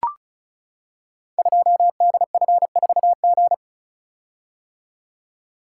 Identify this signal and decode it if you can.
Morse code, heard as a keyed tone, about 36 wpm, 700 Hz. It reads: 2DF4G
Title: Code Groups - 5 Letter:Numbers - 35wpm